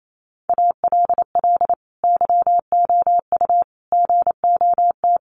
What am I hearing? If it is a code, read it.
Morse code, heard as a keyed tone, about 28 wpm, 700 Hz. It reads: ALL YOU GOT
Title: A Sound Of Thunder 28wpm